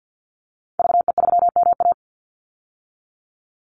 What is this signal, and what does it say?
Morse code, heard as a keyed tone, about 50 wpm, 700 Hz. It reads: VE3MA
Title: Call Signs - 50wpm